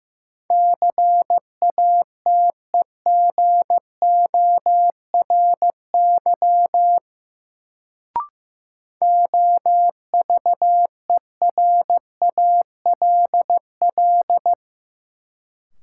Morse code, heard as a keyed tone, about 15 wpm, 700 Hz. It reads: CATEGORY OVERALL
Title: Sentences from Top 1000 Words - Review Part 7 - New Word Drill - 15wpm